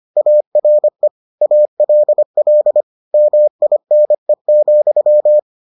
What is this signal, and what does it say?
Morse code, heard as a keyed tone, about 25 wpm, 600 Hz. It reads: ARE ALL MINE,